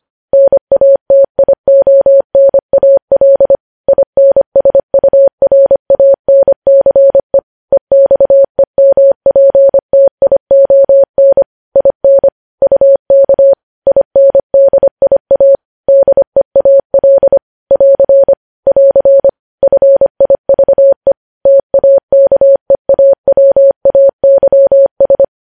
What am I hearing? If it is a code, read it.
Morse code, heard as a keyed tone, about 25 wpm, 550 Hz. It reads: NATIONAL INSURANCE EXEMPTION IN UK INDIA DEAL <AR> <AR> FIVE TAKEAWAYS
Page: news headlines in morse code at 25 wpm